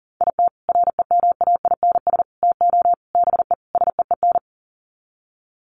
Morse code, heard as a keyed tone, about 40 wpm, 700 Hz. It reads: IT REMAINS TO BE SEEN
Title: Sets of 5 Words Encore 40wpm